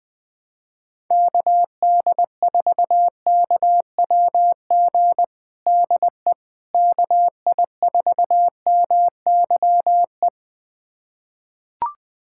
Morse code, heard as a keyed tone, about 20 wpm, 700 Hz. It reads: KD4KWG DE KI4MYE
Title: USA Callsign DE Callsign 20wpm